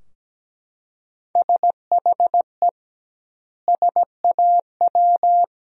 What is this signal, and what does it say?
Morse code, heard as a keyed tone, about 17 wpm, 700 Hz. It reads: SHE SAW
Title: 2x Word-Spacing - Sentences from Top 300 Words - 17wpm